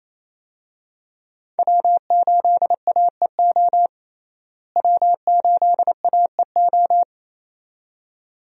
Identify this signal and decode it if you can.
Morse code, heard as a keyed tone, about 28 wpm, 700 Hz. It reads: W8AEO W8AEO
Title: USA Call Signs - 5 Characters - Repeated 2x - 28wpm